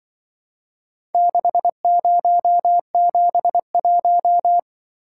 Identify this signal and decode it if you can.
Morse code, heard as a keyed tone, about 23 wpm, 700 Hz. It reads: 6071
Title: Speed-Racing - Four Letter-Number Combinations - 20wpm